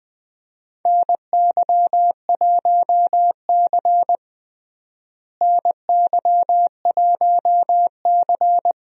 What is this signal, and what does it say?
Morse code, heard as a keyed tone, about 20 wpm, 700 Hz. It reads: NY1C NY1C
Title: USA Call Signs - 4 Characters - Repeated 2x - 20wpm